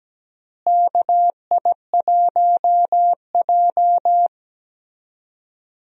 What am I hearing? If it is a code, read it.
Morse code, heard as a keyed tone, about 17 wpm, 700 Hz. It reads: KI1J
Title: USA Call Signs - 4 Characters - Repeated 2x - 17wpm